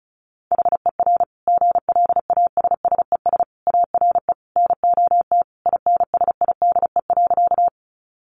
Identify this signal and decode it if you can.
Morse code, heard as a keyed tone, about 34 wpm, 700 Hz. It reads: HER GLASSES ARE NOT INSIDE.
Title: Additional Easy Phrases 35wpm